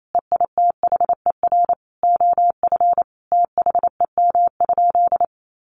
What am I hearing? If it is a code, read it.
Morse code, heard as a keyed tone, about 28 wpm, 700 Hz. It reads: EITHER OF THEM?
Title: Sentences from Top 600 Words 28wpm